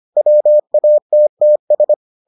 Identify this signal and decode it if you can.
Morse code, heard as a keyed tone, about 25 wpm, 600 Hz. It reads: WATTS